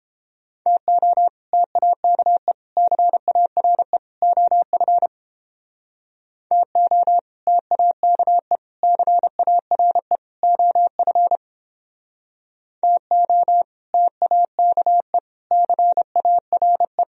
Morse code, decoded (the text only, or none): TO TAKE CARE OF TO TAKE CARE OF TO TAKE CARE